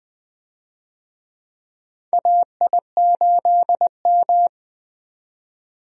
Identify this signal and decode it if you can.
Morse code, heard as a keyed tone, about 21 wpm, 700 Hz. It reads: AI8M
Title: USA Call Signs - 4 Characters - 20wpm